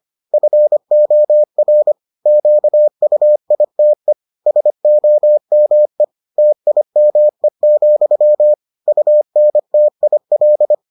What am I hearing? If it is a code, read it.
Morse code, heard as a keyed tone, about 25 wpm, 600 Hz. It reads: FOR QUITE SOME TIME, UNTIL